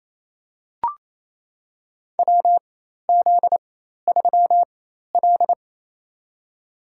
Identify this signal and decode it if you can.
Morse code, heard as a keyed tone, about 28 wpm, spaced as for 13 wpm, 700 Hz. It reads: WZ3L
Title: USA Call Signs - 4 Characters - 4x Character Spacing - 28wpm